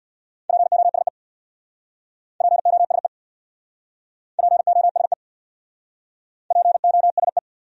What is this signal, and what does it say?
Morse code, heard as a keyed tone, about 60 wpm, 700 Hz. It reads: POSE POSE POSE POSE
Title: Speed-Racing Top 500 Words 50wpm